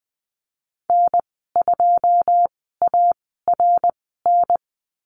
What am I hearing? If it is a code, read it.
Morse code, heard as a keyed tone, about 20 wpm, spaced as for 14 wpm, 700 Hz. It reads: N2ARN
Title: 088 Lesson 15g - Number 5 - Call Signs 20wpm